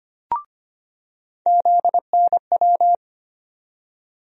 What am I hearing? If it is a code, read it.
Morse code, heard as a keyed tone, about 25 wpm, 700 Hz. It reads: ZNW
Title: USA Call Sign Postfixes - 3 Letters 25wpm